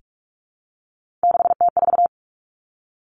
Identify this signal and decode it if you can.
Morse code, heard as a keyed tone, about 46 wpm, 700 Hz. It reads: 6T4
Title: Three Challenging Letter-Number Combinations 45wpm